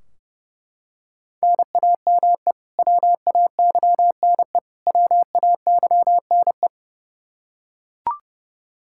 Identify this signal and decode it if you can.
Morse code, heard as a keyed tone, about 30 wpm, 700 Hz. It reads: NAME WAYNE WAYNE
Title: Practice QSO - Exchange 1 - Beginning - 30wpm